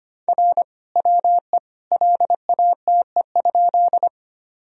Morse code, heard as a keyed tone, about 25 wpm, 700 Hz. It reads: R WE LATE?
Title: Sentences from Top 300 Words and Abbreviations 25wpm